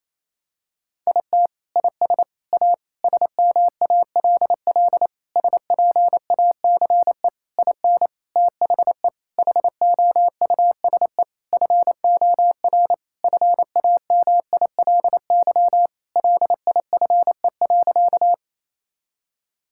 Morse code, heard as a keyed tone, about 27 wpm, 700 Hz. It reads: IT IS A SMALL SPACE IN THE HOUSE FOR FAMILY LIFE.
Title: Sentences from Top 1000 Words - Review Part 5 - 28wpm